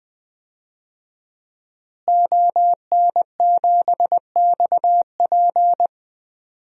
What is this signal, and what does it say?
Morse code, heard as a keyed tone, about 20 wpm, 700 Hz. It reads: ON7XP
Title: DXCC-100 Call Signs - 5 Characters - Repeated 2x - 20wpm